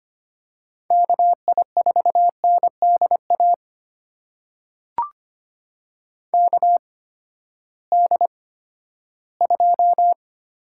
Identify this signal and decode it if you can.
Morse code, heard as a keyed tone, about 25 wpm, spaced as for 14 wpm, 700 Hz. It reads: KI4NDA KD2
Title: USA Call Signs - 6 Characters - 8x Character Spacing - 25wpm